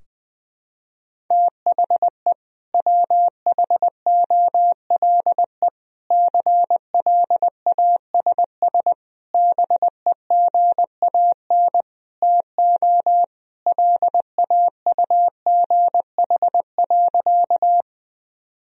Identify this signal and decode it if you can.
Morse code, heard as a keyed tone, about 20 wpm, 700 Hz. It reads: THE WHOLE CLASS BEGAN TO LAUGH.
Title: Sentences from Top 1000 Words - Review Part 2 - 20wpm